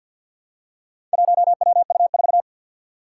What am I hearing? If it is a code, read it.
Morse code, heard as a keyed tone, about 52 wpm, 700 Hz. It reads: 1WU4
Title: Four Challenging Letter-Number Combinations 50wpm